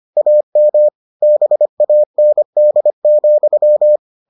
Morse code, heard as a keyed tone, about 25 wpm, 600 Hz. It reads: AM BAND,